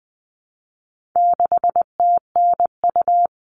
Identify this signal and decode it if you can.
Morse code, heard as a keyed tone, about 20 wpm, 700 Hz. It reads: 6TNU
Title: All Cut Numbers - 4-Digit 20wpm